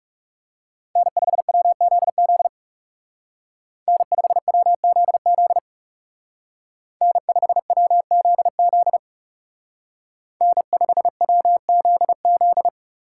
Morse code, heard as a keyed tone, about 35 wpm, 700 Hz. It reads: N5WZZ N5WZZ N5WZZ N5WZZ
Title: Speed Racing - USA 1x3 Call Signs 30wpm